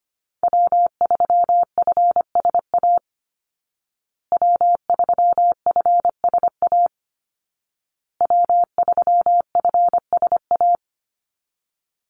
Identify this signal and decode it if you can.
Morse code, heard as a keyed tone, about 25 wpm, 700 Hz. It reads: W3FSA W3FSA W3FSA
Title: Call Signs Repeated 2x - 25wpm